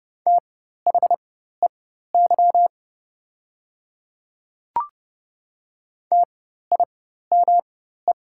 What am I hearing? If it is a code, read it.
Morse code, heard as a keyed tone, about 30 wpm, spaced as for 14 wpm, 700 Hz. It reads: THEY TIME
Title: 4x Character Spacing - Top 500 Words - 30wpm